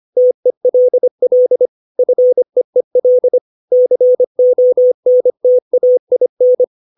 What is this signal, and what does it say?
Morse code, heard as a keyed tone, about 25 wpm, 500 Hz. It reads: TELL FEEL CONTAIN
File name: 632 words / 632 Words 25WPM